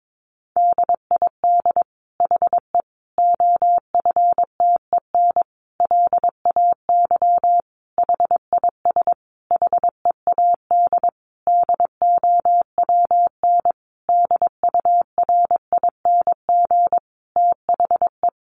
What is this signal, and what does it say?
Morse code, heard as a keyed tone, about 22 wpm, 700 Hz. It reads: DID HE OFTEN LAY HIS HEAD DOWN DURING THE
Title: Sentences from Top 1000 Words 22wpm